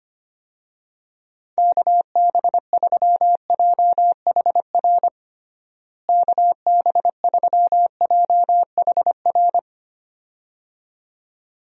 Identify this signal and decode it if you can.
Morse code, heard as a keyed tone, about 25 wpm, 700 Hz. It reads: KB3JHR KB3JHR
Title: USA Call Signs - 6 Characters - Repeated 2x - 25wpm